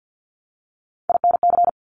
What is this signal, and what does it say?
Morse code, heard as a keyed tone, about 50 wpm, 700 Hz. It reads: INC
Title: Top Three Letter Combinations in Words 50wpm